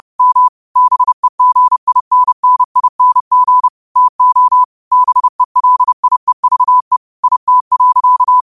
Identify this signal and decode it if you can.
Morse code, heard as a keyed tone, about 30 wpm, 1000 Hz. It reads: M BEGINNING TO BELIEVE IT.
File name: qotdfiles / 30wpm